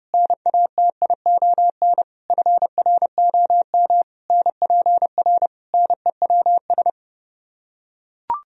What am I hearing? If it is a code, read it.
Morse code, heard as a keyed tone, about 30 wpm, 700 Hz. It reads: NATION FROM NPR NEWS